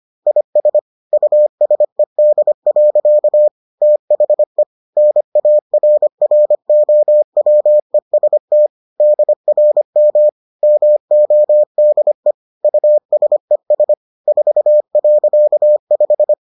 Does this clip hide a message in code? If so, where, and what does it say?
Yes: Morse code from 0.3 to 16.3 s, heard as a keyed tone, about 25 wpm, 600 Hz. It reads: IS USED. THE NARROWEST DRM MODE USES 4.5